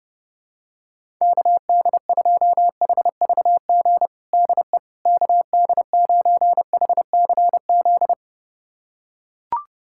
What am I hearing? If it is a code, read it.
Morse code, heard as a keyed tone, about 30 wpm, 700 Hz. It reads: KD2HVG DE KD9HCZ
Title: USA Callsign DE Callsign - 6 Characters - 30wpm